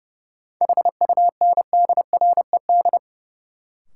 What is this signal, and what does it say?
Morse code, heard as a keyed tone, about 30 wpm, 700 Hz. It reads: HUNDRED
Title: Sentences from Top 400 Words - New Word Drill - 30wpm